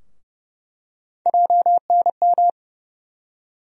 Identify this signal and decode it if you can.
Morse code, heard as a keyed tone, about 31 wpm, 700 Hz. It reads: JNM
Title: USA Call Sign Postfixes - 3 Letters 30wpm